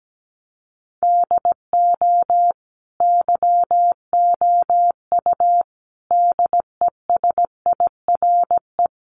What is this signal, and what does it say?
Morse code, heard as a keyed tone, about 17 wpm, 700 Hz. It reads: DO YOU DESIRE
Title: Sentences from Top 1100 Words 17wpm